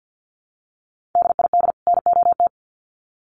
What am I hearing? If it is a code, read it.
Morse code, heard as a keyed tone, about 50 wpm, 700 Hz. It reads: DID NOT
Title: Sets of 2 Words 50wpm